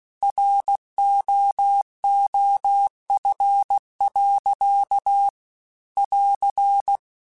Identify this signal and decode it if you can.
Morse code, heard as a keyed tone, about 16 wpm, 800 Hz. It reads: ROOF. <AR>
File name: Weather Sets / Weather Sets-mp3s / Weather 4 28 @15